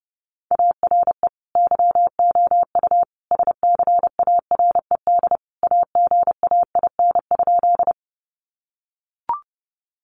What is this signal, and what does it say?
Morse code, heard as a keyed tone, about 30 wpm, 700 Hz. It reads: ARE YOU SCARED AGAIN?
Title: Sentences from Top 700 Words 30wpm